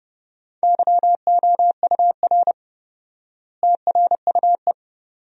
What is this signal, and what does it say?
Morse code, heard as a keyed tone, about 30 wpm, 700 Hz. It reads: YOUR TRUE
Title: 4x Word-Spacing - Sentences from Top 400 Words - 30wpm